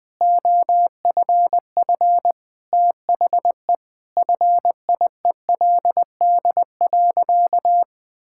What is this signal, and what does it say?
Morse code, heard as a keyed tone, about 20 wpm, 700 Hz. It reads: OFF THE FIELD.
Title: Sentences with 6-Word Phrases - 20wpm